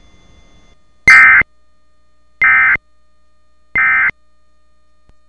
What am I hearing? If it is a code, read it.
Emergency Alert System (SAME) tones, heard as harsh buzzing bursts. It is heard from 1.1 to 4.1 s.